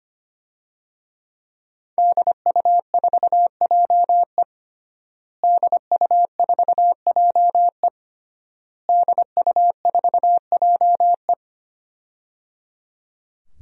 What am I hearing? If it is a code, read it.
Morse code, heard as a keyed tone, about 25 wpm, 700 Hz. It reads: DU4JE DU4JE DU4JE
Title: DXCC-100 Call Signs - 5 Characters - Repeated 3x - 25wpm